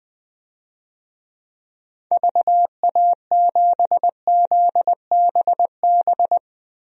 Morse code, heard as a keyed tone, about 20 wpm, 700 Hz. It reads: VA7ZBB
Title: DXCC-100 Call Signs - 6 Characters - Repeated 2x - 20wpm